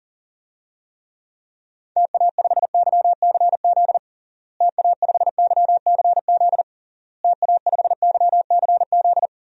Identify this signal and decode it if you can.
Morse code, heard as a keyed tone, about 40 wpm, 700 Hz. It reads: TA5YCZ TA5YCZ TA5YCZ
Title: DXCC-100 Call Signs - 6 Characters - Repeated 3x - 40wpm